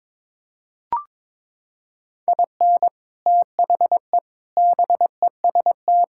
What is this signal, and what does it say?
Morse code, heard as a keyed tone, about 22 wpm, 700 Hz. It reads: IN THE BEST
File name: practice / 22wpm / Sets of 6 Words - 22wpm